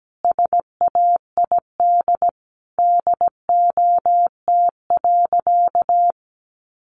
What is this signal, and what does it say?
Morse code, heard as a keyed tone, about 17 wpm, 700 Hz. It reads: SAID DOT.
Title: Sentences from Top 1500 Words 17wpm